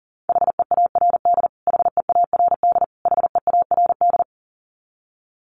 Morse code, heard as a keyed tone, about 40 wpm, 700 Hz. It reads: HEARD HEARD HEARD
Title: Top 500 Words Repeated 40wpm